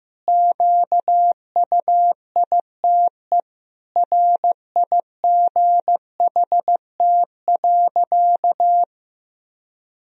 Morse code, heard as a keyed tone, about 15 wpm, 700 Hz. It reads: QUITE RIGHT.
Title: Sentences from Top 1000 Words - Review Part 2 - 15wpm